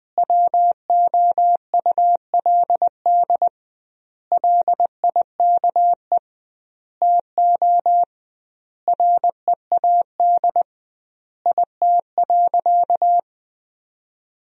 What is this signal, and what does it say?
Morse code, heard as a keyed tone, about 20 wpm, 700 Hz. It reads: WOULD LIKE TO READ IT.
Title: 2x Word-Spacing - Sentences from Top 500 Words - 20wpm